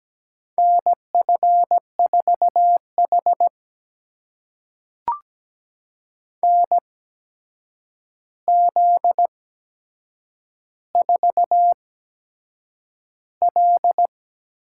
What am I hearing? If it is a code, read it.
Morse code, heard as a keyed tone, about 17 wpm, spaced as for 7 wpm, 700 Hz. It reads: NF4H NZ4L
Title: USA Call Signs - 4 Characters - 8x Character Spacing - 17wpm